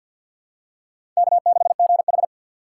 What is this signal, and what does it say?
Morse code, heard as a keyed tone, about 47 wpm, 700 Hz. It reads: K6GH
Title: USA Call Signs - 4 Characters - 3x Character Spacing - 50wpm